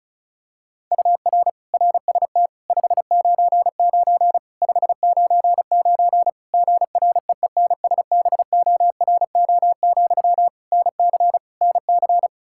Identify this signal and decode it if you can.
Morse code, heard as a keyed tone, about 36 wpm, 700 Hz. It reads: UR RST 599 599 GREENSBORO, NC NC
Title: Practice QSO - Exchange 1 - Beginning - 35wpm